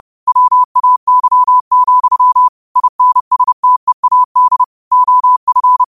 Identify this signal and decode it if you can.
Morse code, heard as a keyed tone, about 30 wpm, 1000 Hz. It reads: WAY, INSTEAD OF
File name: qotdfiles / 30wpm